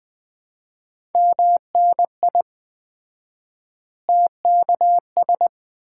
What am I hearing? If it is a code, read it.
Morse code, heard as a keyed tone, about 20 wpm, 700 Hz. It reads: MNI TKS
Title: Common QSO Phrases - 4x Word-Spacing - 20wpm